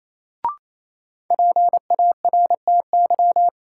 Morse code, heard as a keyed tone, about 28 wpm, 700 Hz. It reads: PARTY
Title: Sentences from Top 1400 Words - New Word Drill Alt - 28wpm